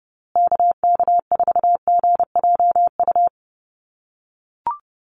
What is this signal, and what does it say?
Morse code, heard as a keyed tone, about 30 wpm, 700 Hz. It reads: KK4GJU
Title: USA Call Signs Repeated 3x 30wpm